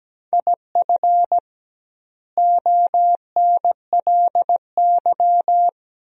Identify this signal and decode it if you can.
Morse code, heard as a keyed tone, about 17 wpm, 700 Hz. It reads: IF ONLY
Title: 2x Word-Spacing - Sentences from Top 500 Words - 17wpm